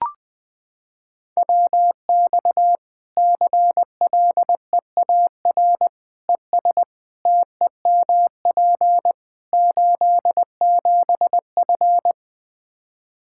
Morse code, heard as a keyed tone, about 20 wpm, 700 Hz. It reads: WX CLEAR ES TEMP 87F
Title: Practice QSO - Exchange 2 - Info - 20wpm